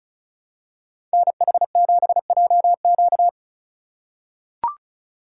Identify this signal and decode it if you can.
Morse code, heard as a keyed tone, about 34 wpm, 700 Hz. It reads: NH7JQ
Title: USA Call Signs - 5 Characters - 2x Character Spacing - 35wpm